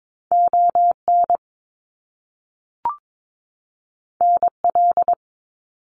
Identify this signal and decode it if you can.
Morse code, heard as a keyed tone, about 22 wpm, 700 Hz. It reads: ON NL
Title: Canadian Provinces and Territories 22wpm